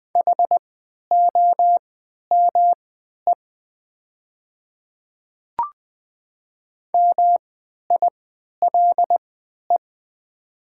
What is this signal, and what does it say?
Morse code, heard as a keyed tone, about 20 wpm, spaced as for 11 wpm, 700 Hz. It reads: HOME MILE
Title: 3x Character Spacing - Top 300 Words - 20wpm